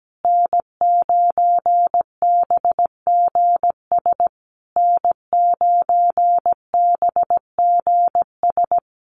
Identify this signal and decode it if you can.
Morse code, heard as a keyed tone, about 17 wpm, 700 Hz. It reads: N9BGS N9BGS
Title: USA Call Signs Repeated 3x 17wpm